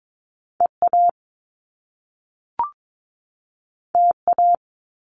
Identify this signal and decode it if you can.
Morse code, heard as a keyed tone, about 22 wpm, 700 Hz. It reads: EA TA
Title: Top Two Letter Combinations in Words 22wpm